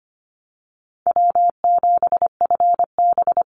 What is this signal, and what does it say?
Morse code, heard as a keyed tone, about 25 wpm, 700 Hz. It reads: W7FB
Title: Call Signs - 25wpm